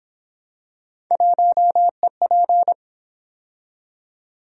Morse code, heard as a keyed tone, about 29 wpm, 700 Hz. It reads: 1EP